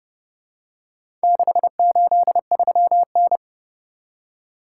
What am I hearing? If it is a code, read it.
Morse code, heard as a keyed tone, about 29 wpm, 700 Hz. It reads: 683N